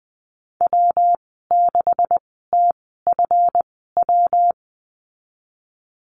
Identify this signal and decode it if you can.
Morse code, heard as a keyed tone, about 20 wpm, spaced as for 14 wpm, 700 Hz. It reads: W6TFW
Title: 219 Lesson 35g - Number 0 - Call Signs 20wpm